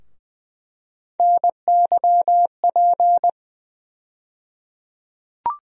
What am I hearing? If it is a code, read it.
Morse code, heard as a keyed tone, about 20 wpm, 700 Hz. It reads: NYP
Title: USA Call Sign Postfixes - 3 Letters 20wpm